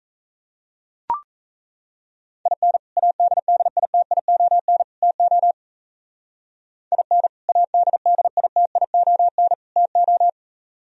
Morse code, heard as a keyed tone, about 32 wpm, 700 Hz. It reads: IN ADDITION TO IN ADDITION TO
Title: Speed-Racing - Sets of 3 Words 28wpm